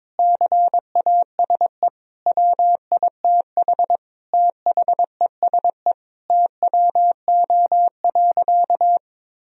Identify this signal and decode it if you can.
Morse code, heard as a keyed tone, about 22 wpm, 700 Hz. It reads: CASE WITH THESE TWO.
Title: Sentences from Top 1000 Words - Review Part 1 - 22wpm